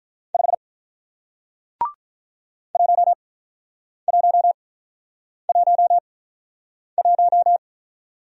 Morse code, heard as a keyed tone, about 44 wpm, spaced as for 9 wpm, 700 Hz. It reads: 5 1111
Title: Speed-Racing Single Letter-Number 35wpm